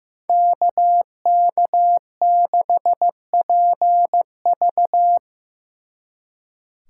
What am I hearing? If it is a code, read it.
Morse code, heard as a keyed tone, about 18 wpm, 700 Hz. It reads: KK6PV
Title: Speed Racing - USA 2x2 Call Signs 15wpm